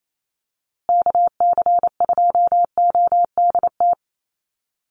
Morse code, heard as a keyed tone, about 28 wpm, 700 Hz. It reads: KC2ODT
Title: USA Call Signs Repeated 2x 28wpm